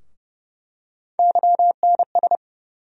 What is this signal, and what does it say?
Morse code, heard as a keyed tone, about 30 wpm, 700 Hz. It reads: YNS
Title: USA Call Sign Postfixes - 3 Letters 30wpm